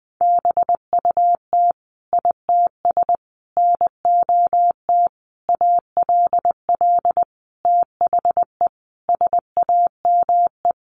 Morse code, heard as a keyed tone, about 20 wpm, 700 Hz. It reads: BUT ITS NOT ALL THE SAME
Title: Sentences from Top 600 Words 20wpm